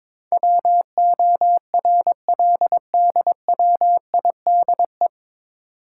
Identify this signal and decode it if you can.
Morse code, heard as a keyed tone, about 22 wpm, 700 Hz. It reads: WORLDWIDE
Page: Morse Code Ninja practice 22wpm